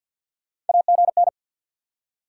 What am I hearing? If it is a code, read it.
Morse code, heard as a keyed tone, about 50 wpm, 700 Hz. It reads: AGN
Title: SOTA QSO Practice 50wpm